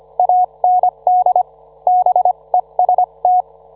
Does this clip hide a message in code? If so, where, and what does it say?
Yes: Morse code from 0.2 to 3.4 s, heard as a keyed tone, about 25 wpm, 730 Hz. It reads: AND BEST